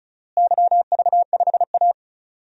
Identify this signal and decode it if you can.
Morse code, heard as a keyed tone, about 34 wpm, 700 Hz. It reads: YV5A
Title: DXCC-100 Call Signs - 4 Characters - Repeated 2x - 35wpm